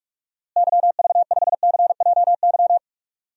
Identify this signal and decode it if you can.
Morse code, heard as a keyed tone, about 46 wpm, 700 Hz. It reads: YV5CJY
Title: DXCC-100 Call Signs - 6 Characters - Repeated 2x - 45wpm